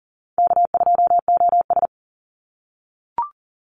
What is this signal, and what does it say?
Morse code, heard as a keyed tone, about 40 wpm, 700 Hz. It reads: K2OS
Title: Easy Call Signs - 40wpm